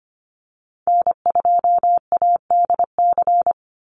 Morse code, heard as a keyed tone, about 25 wpm, 700 Hz. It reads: N2ADC
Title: Call Signs Repeated 3x - 25wpm